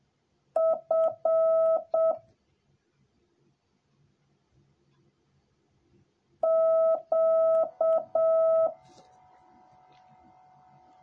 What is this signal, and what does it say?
Morse code, heard as a keyed tone, about 7 wpm, 650 Hz. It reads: F Q